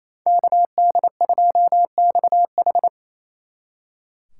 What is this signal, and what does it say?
Morse code, heard as a keyed tone, about 35 wpm, 700 Hz. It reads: KD2XH